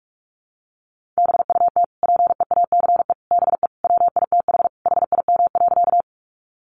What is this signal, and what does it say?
Morse code, heard as a keyed tone, about 44 wpm, 700 Hz. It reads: BUT PEACE BE WITH HIM.
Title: Sentences from Top 1500 Words 45wpm